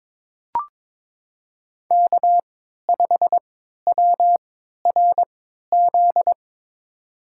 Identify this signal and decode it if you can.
Morse code, heard as a keyed tone, about 23 wpm, spaced as for 13 wpm, 700 Hz. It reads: K5WRZ
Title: USA Call Signs - 5 Characters - 3x Character Spacing - 22wpm